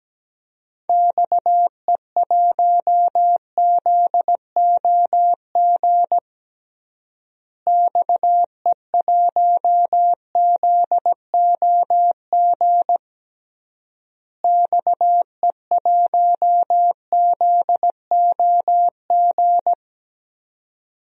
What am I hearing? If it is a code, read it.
Morse code, heard as a keyed tone, about 17 wpm, 700 Hz. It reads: XE1ZOG XE1ZOG XE1ZOG
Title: DXCC-100 Call Signs - 6 Characters - Repeated 3x - 17wpm